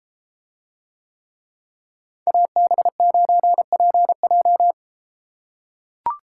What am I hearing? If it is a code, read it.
Morse code, heard as a keyed tone, about 32 wpm, 700 Hz. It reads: AB9PJ